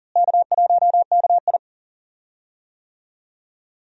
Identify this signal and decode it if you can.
Morse code, heard as a keyed tone, about 37 wpm, 700 Hz. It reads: K1KI
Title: USA Call Signs - 4 Characters - Repeated 2x - 40wpm